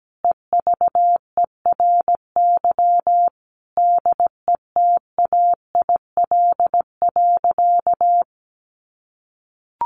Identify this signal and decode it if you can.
Morse code, heard as a keyed tone, about 17 wpm, 700 Hz. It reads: EVERY DETAIL.
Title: Sentences from Top 1500 Words 17wpm